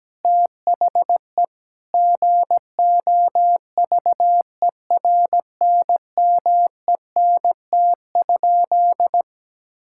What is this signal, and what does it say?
Morse code, heard as a keyed tone, about 17 wpm, 700 Hz. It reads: THE GOVERNMENT?
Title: Sentences from Top 1000 Words - Review Part 5 - 17wpm